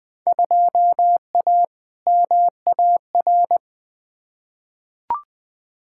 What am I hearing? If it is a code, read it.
Morse code, heard as a keyed tone, about 20 wpm, 700 Hz. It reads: 2A MAR
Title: Field Day Exchanges - 20wpm